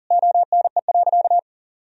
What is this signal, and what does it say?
Morse code, heard as a keyed tone, about 40 wpm, 700 Hz. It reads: ONE.
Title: Sentences from Top 1000 Words - Review Part 2 - 40wpm